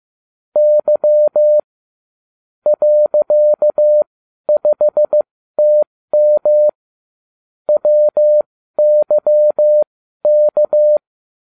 Morse code, heard as a keyed tone, about 15 wpm, spaced as for 12 wpm, 600 Hz. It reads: Y .5TM WYK